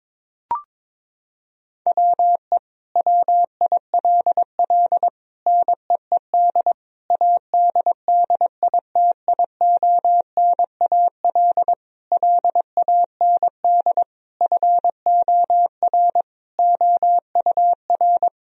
Morse code, heard as a keyed tone, about 22 wpm, 700 Hz. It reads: WE WILL NEED ADDITIONAL LAND FOR OUR
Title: Sentences from Top 1000 Words - Review Part 4 - 22wpm